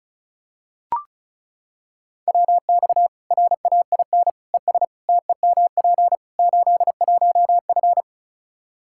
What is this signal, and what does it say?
Morse code, heard as a keyed tone, about 36 wpm, 700 Hz. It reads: WX RAIN ES TEMP 81F
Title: Practice QSO - Exchange 2 - Info - 35wpm